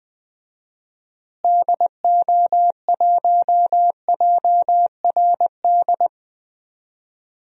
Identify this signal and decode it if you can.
Morse code, heard as a keyed tone, about 21 wpm, 700 Hz. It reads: DO1JRD
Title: DXCC-100 Call Signs - 6 Characters - Repeated 3x - 20wpm